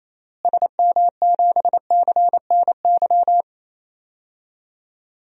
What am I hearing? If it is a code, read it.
Morse code, heard as a keyed tone, about 28 wpm, 700 Hz. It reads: SM7CNY
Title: DXCC-100 Call Signs - 6 Characters - Repeated 3x - 28wpm